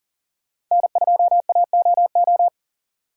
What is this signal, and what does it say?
Morse code, heard as a keyed tone, about 40 wpm, 700 Hz. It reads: N2AOO
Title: USA Call Signs - 6x Character Spacing - 40wpm